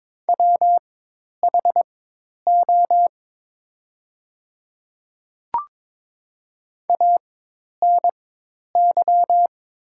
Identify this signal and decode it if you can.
Morse code, heard as a keyed tone, about 22 wpm, spaced as for 10 wpm, 700 Hz. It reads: WHO ANY
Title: 4x Character Spacing - Top 300 Words - 22wpm